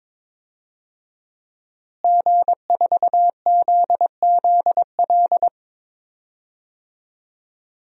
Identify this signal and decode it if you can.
Morse code, heard as a keyed tone, about 21 wpm, 700 Hz. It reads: G4ZZL
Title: DXCC-100 Call Signs - 5 Characters - Repeated 3x - 22wpm